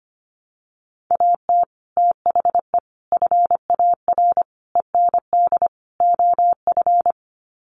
Morse code, heard as a keyed tone, about 25 wpm, 700 Hz. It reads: AT THE FAR END OF
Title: Sets of 5 Words 25wpm